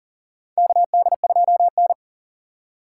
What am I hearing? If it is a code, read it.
Morse code, heard as a keyed tone, about 40 wpm, 700 Hz. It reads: KD2N
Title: USA Call Signs - 4 Characters - 4x Character Spacing - 40wpm